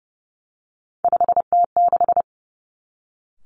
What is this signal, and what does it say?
Morse code, heard as a keyed tone, about 32 wpm, 700 Hz. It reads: 5T6
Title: Three Challenging Letter-Number Combinations 30wpm